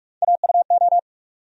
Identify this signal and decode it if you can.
Morse code, heard as a keyed tone, about 42 wpm, 700 Hz. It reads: AUO
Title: USA Call Sign Postfixes - 3 Letters 45wpm